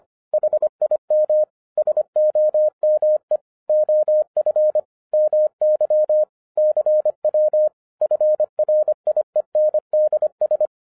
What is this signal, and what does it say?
Morse code, heard as a keyed tone, about 25 wpm, 610 Hz. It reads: HIM SOME OF MY CW FRIENDS